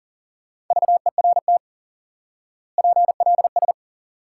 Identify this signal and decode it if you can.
Morse code, heard as a keyed tone, about 37 wpm, 700 Hz. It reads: VERT PLS
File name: IQE Course 3 - Lesson 89 - YAGI - Two Elements - 40wpm